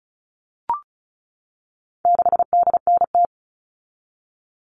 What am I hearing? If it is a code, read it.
Morse code, heard as a keyed tone, about 34 wpm, 700 Hz. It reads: 6DNT
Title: All Cut Numbers - 4-Digit 35wpm